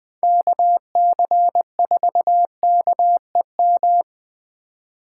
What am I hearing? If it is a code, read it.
Morse code, heard as a keyed tone, about 24 wpm, 700 Hz. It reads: KC4KEM
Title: Speed Racing - USA 2x3 Call Signs 20wpm